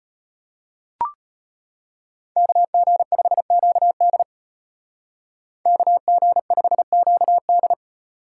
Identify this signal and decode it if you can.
Morse code, heard as a keyed tone, about 31 wpm, 700 Hz. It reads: KG5QD KG5QD
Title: Speed Racing - USA 2x2 Call Signs 25wpm